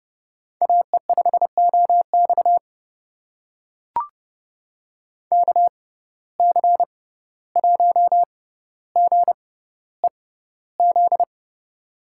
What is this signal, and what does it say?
Morse code, heard as a keyed tone, about 31 wpm, spaced as for 15 wpm, 700 Hz. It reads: AE5OX KC1GEZ
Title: USA Call Signs - 6x Character Spacing - 30wpm